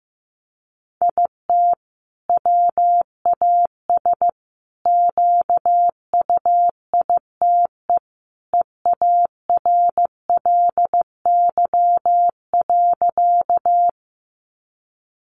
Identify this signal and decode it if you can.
Morse code, heard as a keyed tone, about 15 wpm, 700 Hz. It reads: IT WAS QUITE EARLY.
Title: Sentences from Top 700 Words 15wpm